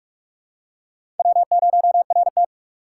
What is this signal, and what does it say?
Morse code, heard as a keyed tone, about 44 wpm, 700 Hz. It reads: W0RT
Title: USA Call Signs - 6x Character Spacing - 45wpm